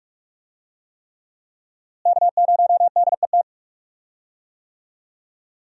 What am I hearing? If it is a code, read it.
Morse code, heard as a keyed tone, about 43 wpm, 700 Hz. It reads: K0DET
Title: USA Call Signs - 5 Characters - Repeated 3x - 45wpm